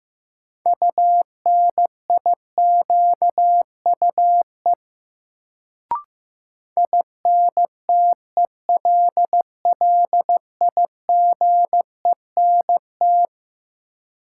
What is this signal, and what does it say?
Morse code, heard as a keyed tone, about 16 wpm, 700 Hz. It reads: UNIQUE INTELLIGENT
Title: Sentences from Top 1000 Words - Review Part 7 - New Word Drill - 15wpm